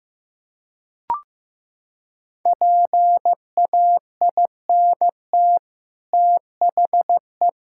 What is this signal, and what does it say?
Morse code, heard as a keyed tone, about 14 wpm, 700 Hz. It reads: PAINT THE
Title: Sentences from Top 1000 Words - Review Part 6 - 15wpm